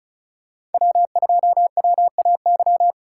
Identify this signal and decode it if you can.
Morse code, heard as a keyed tone, about 36 wpm, 700 Hz. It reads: W2WAY
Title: USA Call Signs - 5 Characters - 4x Character Spacing - 35wpm